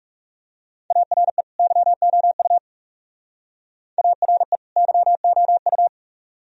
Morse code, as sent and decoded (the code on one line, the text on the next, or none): .- .-. / -.-- --- ..- / .- .-. / -.-- --- ..-
ARE YOU ARE YOU